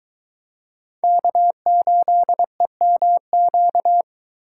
Morse code, heard as a keyed tone, about 23 wpm, 700 Hz. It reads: K8EMQ